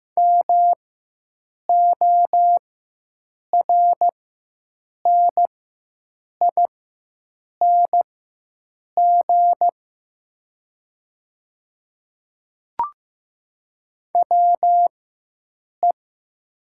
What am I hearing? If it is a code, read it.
Morse code, heard as a keyed tone, about 15 wpm, spaced as for 7 wpm, 700 Hz. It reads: MORNING WE
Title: 4x Character Spacing - Top 500 Words - 15wpm